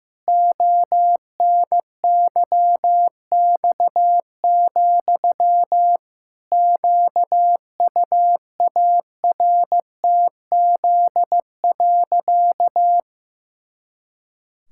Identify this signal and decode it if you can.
Morse code, heard as a keyed tone, about 15 wpm, 700 Hz. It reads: ONYX, QUARTZ.
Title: Pangram Challenge v2 - 15wpm